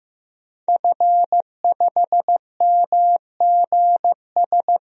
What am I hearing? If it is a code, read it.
Morse code, heard as a keyed tone, about 15 wpm, 700 Hz. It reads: F5MGS
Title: DXCC-100 Call Signs - 5 Characters - Repeated 3x - 15wpm